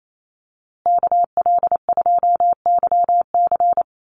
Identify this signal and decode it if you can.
Morse code, heard as a keyed tone, about 28 wpm, 700 Hz. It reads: KL2YC
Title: USA Call Signs Repeated 3x 28wpm